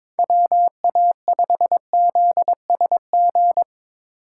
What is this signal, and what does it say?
Morse code, heard as a keyed tone, about 22 wpm, 700 Hz. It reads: WA5ZSG
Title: USA Call Signs - 3x Character Spacing - 22wpm